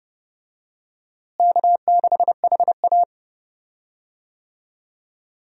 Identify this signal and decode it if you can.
Morse code, heard as a keyed tone, about 30 wpm, 700 Hz. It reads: K6HA
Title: USA Call Signs - 4 Characters - Repeated 3x - 30wpm